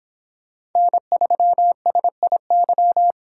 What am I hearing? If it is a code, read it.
Morse code, heard as a keyed tone, about 25 wpm, 700 Hz. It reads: N3SIY